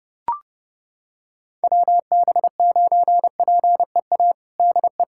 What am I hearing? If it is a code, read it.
Morse code, heard as a keyed tone, about 32 wpm, 700 Hz. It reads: WB9PEA DE
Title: USA Callsign DE Callsign - 6 Characters - 30wpm